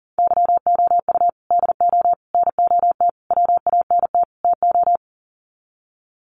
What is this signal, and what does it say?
Morse code, heard as a keyed tone, about 40 wpm, 700 Hz. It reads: YOU DO NOT WANT TO
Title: Sets of 5 Words 40wpm